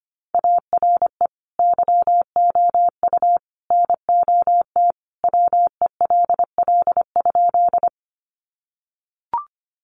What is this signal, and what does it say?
Morse code, heard as a keyed tone, about 25 wpm, 700 Hz. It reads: ARE YOU NOT WELL?
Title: Sentences from Top 600 Words 25wpm